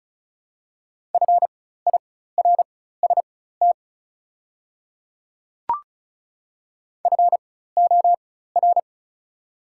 Morse code, heard as a keyed tone, about 36 wpm, spaced as for 16 wpm, 700 Hz. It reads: FIRST FOR
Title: 4x Character Spacing - Top 100 Words - 35wpm